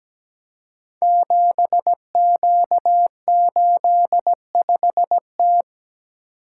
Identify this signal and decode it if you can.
Morse code, heard as a keyed tone, about 17 wpm, 700 Hz. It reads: 7Q85T
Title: Code Groups - 5 Letter:Numbers - 17wpm